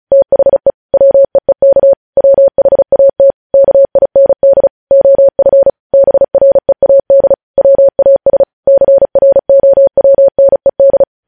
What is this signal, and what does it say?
Morse code, heard as a keyed tone, about 36 wpm, 550 Hz. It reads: THE WEEK WHAT KIND OF BREAD WAS CROWNED
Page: News headlines in Morse code at 35 wpm